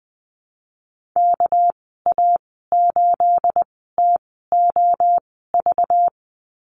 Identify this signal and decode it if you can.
Morse code, heard as a keyed tone, about 20 wpm, spaced as for 14 wpm, 700 Hz. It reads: KA8TOV
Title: 238 Lesson 38g - Letter q - Call Signs 20wpm